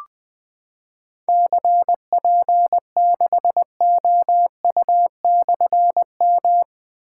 Morse code, heard as a keyed tone, about 20 wpm, 700 Hz. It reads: CP6OU/M
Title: Difficult Call Signs 20wpm